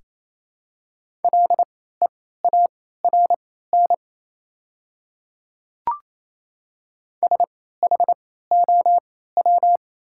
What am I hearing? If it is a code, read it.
Morse code, heard as a keyed tone, about 28 wpm, spaced as for 16 wpm, 700 Hz. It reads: LEARN SHOW
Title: 3x Character Spacing - Top 300 Words - 28wpm